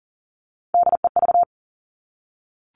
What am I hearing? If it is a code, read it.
Morse code, heard as a keyed tone, about 42 wpm, 700 Hz. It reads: DEV
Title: Three Challenging Letter-Number Combinations 40wpm